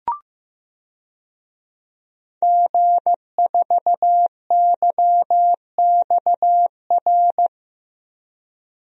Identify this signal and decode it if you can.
Morse code, heard as a keyed tone, about 15 wpm, 700 Hz. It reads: G4YXR
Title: DXCC-100 Call Signs - 5 Characters - Repeated 2x - 15wpm